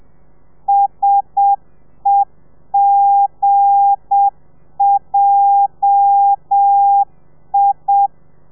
Morse code, heard as a keyed tone, about 7 wpm, 800 Hz. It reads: SEGJI
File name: telegrafia / i2tfj / Lez8 30